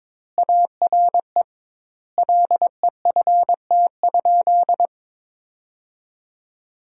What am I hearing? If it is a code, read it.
Morse code, heard as a keyed tone, about 21 wpm, 700 Hz. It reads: ARE LEFT?
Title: 2x Word-Spacing - Sentences from Top 300 Words - 22wpm